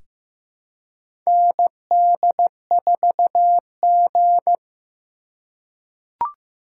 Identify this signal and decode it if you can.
Morse code, heard as a keyed tone, about 15 wpm, 700 Hz. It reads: ND4G
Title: K1USN SST QSO Practice - 15wpm